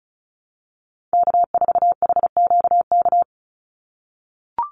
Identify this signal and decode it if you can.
Morse code, heard as a keyed tone, about 34 wpm, 700 Hz. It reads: K4HQK
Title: Call Signs - 35wpm